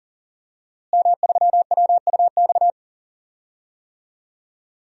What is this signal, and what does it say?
Morse code, heard as a keyed tone, about 40 wpm, 700 Hz. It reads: M3WUX
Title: DXCC-100 Call Signs - 5 Characters - Repeated 3x - 40wpm